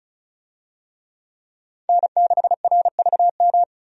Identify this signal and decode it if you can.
Morse code, heard as a keyed tone, about 36 wpm, 700 Hz. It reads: N6RVM
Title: USA Call Signs - 5 Characters - 35wpm